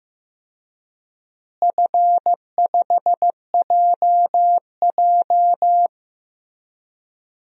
Morse code, heard as a keyed tone, about 15 wpm, 700 Hz. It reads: F5JJ
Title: DXCC-100 Call Signs - 4 Characters - Repeated 3x - 15wpm